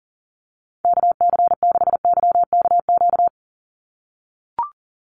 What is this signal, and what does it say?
Morse code, heard as a keyed tone, about 40 wpm, 700 Hz. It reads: KC6YKQ
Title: USA Call Signs 40wpm